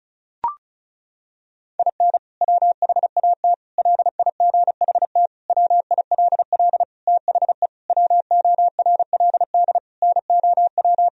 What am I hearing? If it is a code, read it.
Morse code, heard as a keyed tone, about 34 wpm, 700 Hz. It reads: IN WHAT LIGHT WILL THE WORLD NOW
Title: Sentences from Top 1000 Words - Review Part 2 - 35wpm